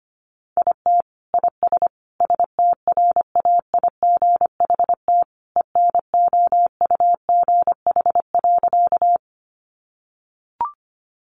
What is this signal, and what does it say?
Morse code, heard as a keyed tone, about 25 wpm, 700 Hz. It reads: IT IS STRAIGHT ENOUGH.
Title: Sentences from Top 1000 Words 25wpm